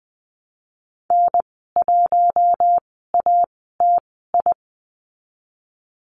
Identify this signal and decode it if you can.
Morse code, heard as a keyed tone, about 20 wpm, spaced as for 14 wpm, 700 Hz. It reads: N1ATI
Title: 082 Lesson 14g - Number 2 - Call Signs 20wpm